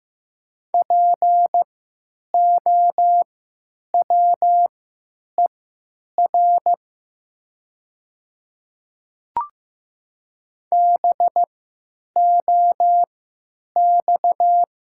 Morse code, heard as a keyed tone, about 15 wpm, spaced as for 9 wpm, 700 Hz. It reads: POWER BOX
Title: 3x Character Spacing - Top 500 Words - 15wpm